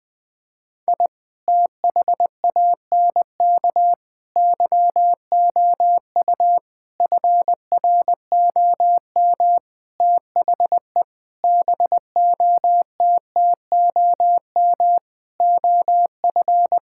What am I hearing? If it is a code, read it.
Morse code, heard as a keyed tone, about 21 wpm, 700 Hz. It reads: I THANK YOU FROM THE BOTTOM OF
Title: Sentences with 6-Word Phrases - 20wpm